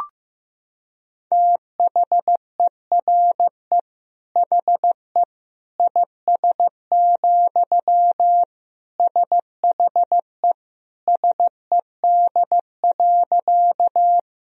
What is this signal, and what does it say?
Morse code, heard as a keyed tone, about 15 wpm, 700 Hz. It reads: THERE HE IS, SHE SED.
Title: Sentences from Top 100 Words and Abbreviations 15wpm